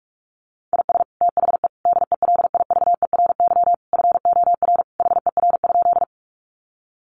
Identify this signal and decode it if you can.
Morse code, heard as a keyed tone, about 44 wpm, 700 Hz. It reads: IS THE DELIVERY FOR HER?
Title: Sentences from Top 1500 Words 45wpm